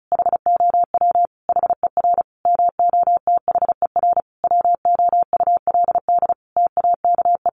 Morse code, heard as a keyed tone, about 34 wpm, 700 Hz. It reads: HOW HER MOTHER WOULD TAKE
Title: Sentences from Top 700 Words 35wpm